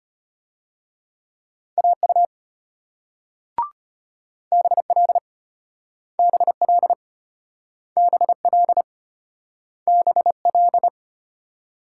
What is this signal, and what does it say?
Morse code, heard as a keyed tone, about 34 wpm, 700 Hz. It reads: AU BL BL BL BL